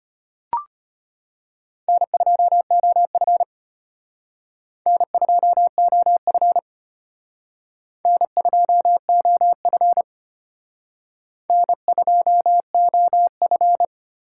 Morse code, heard as a keyed tone, about 32 wpm, 700 Hz. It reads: N2OF N2OF N2OF N2OF